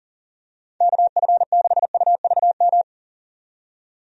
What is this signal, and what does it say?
Morse code, heard as a keyed tone, about 37 wpm, 700 Hz. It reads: KF6UVM
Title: USA Call Signs - 6 Characters - 40wpm